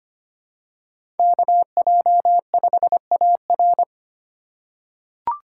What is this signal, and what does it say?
Morse code, heard as a keyed tone, about 25 wpm, 700 Hz. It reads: KJ5AR
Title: USA Call Signs - 5 Characters - 4x Character Spacing - 25wpm